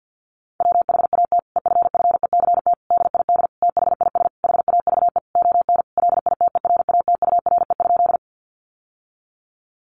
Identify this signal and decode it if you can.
Morse code, heard as a keyed tone, about 48 wpm, 700 Hz. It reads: WHAT EFFECT DID THIS HAVE ON LITERATURE?
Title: Sentences from Top 1000 Words 50wpm